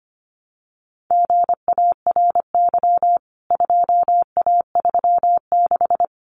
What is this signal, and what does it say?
Morse code, heard as a keyed tone, about 25 wpm, 700 Hz. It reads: GARY 2A36
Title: CWT Exchange 25wpm